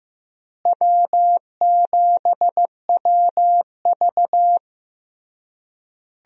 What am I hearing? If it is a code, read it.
Morse code, heard as a keyed tone, about 15 wpm, 700 Hz. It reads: W7WV
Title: USA Call Signs - 4 Characters - Repeated 3x - 15wpm